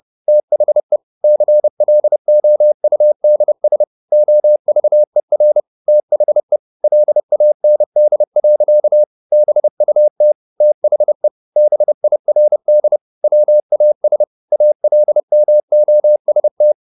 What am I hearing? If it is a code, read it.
Morse code, heard as a keyed tone, about 30 wpm, 600 Hz. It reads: THE CLOUDS OVER THE LAND. BUT THE BIRD WAS ALMOST
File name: the old man and the sea / the old man and the sea 1 30wpm